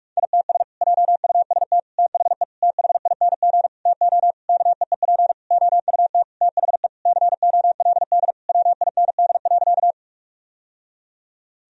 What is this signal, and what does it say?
Morse code, heard as a keyed tone, about 46 wpm, 700 Hz. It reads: ITS JUST THE THING TO KEEP OUT THE COLD WIND.
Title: Sentences from Top 1000 Words - Review Part 2 - 45wpm